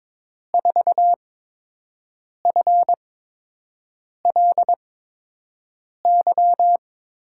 Morse code, heard as a keyed tone, about 22 wpm, spaced as for 6 wpm, 700 Hz. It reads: 4FLY
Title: USA Call Signs - 6 Characters - 8x Character Spacing - 22wpm